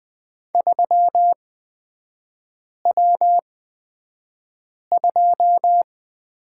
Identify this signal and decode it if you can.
Morse code, heard as a keyed tone, about 20 wpm, spaced as for 5 wpm, 690 Hz. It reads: 3W2
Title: 255 Lesson 41e - Period - Three Characters 20wpm